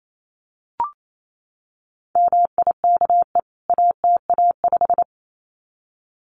Morse code, heard as a keyed tone, about 28 wpm, 700 Hz. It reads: MIKE ATA5
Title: CWT Exchange 28wpm